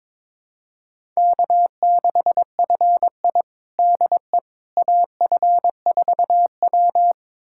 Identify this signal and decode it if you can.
Morse code, heard as a keyed tone, about 21 wpm, 700 Hz. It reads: K6FI DE AF4W
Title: USA Callsign DE Callsign - 4 Characters - 22wpm